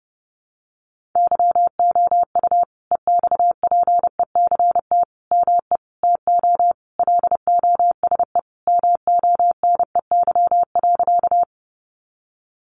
Morse code, heard as a keyed tone, about 30 wpm, 700 Hz. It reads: YOU EXPECT ME TO LOSE MONEY.
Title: Sentences from Top 1100 Words 30wpm